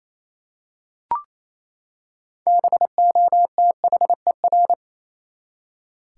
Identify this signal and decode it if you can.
Morse code, heard as a keyed tone, about 28 wpm, 700 Hz. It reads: BOTHER
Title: Six Letter Words - 28wpm